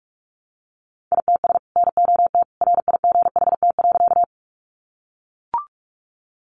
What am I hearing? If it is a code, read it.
Morse code, heard as a keyed tone, about 44 wpm, 700 Hz. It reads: ITS NOT RIGHT.
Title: Sentences from Top 600 Words 45wpm